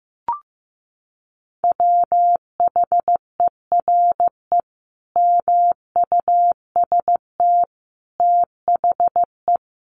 Morse code, heard as a keyed tone, about 15 wpm, 700 Hz. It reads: WHERE MUST THE
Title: Sentences from Top 1300 Words 15wpm